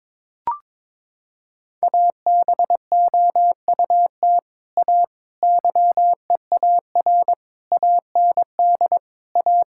Morse code, heard as a keyed tone, about 22 wpm, 700 Hz. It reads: ABOUT A YEAR AND A
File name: practice / 22wpm / Sets of 7 Words - 22wpm